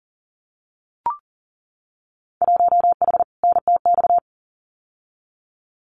Morse code, heard as a keyed tone, about 40 wpm, 700 Hz. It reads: 1H NTX
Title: Winter Field Day Exchanges 40wpm